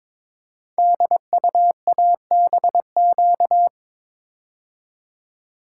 Morse code, heard as a keyed tone, about 23 wpm, 700 Hz. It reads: DUABQ
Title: Code Groups - 5 Letters - 22wpm